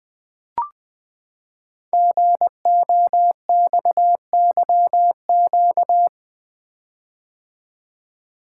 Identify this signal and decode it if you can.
Morse code, heard as a keyed tone, about 20 wpm, 700 Hz. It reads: GOXYQ
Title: Code Groups - 5 Letter:Numbers - 20wpm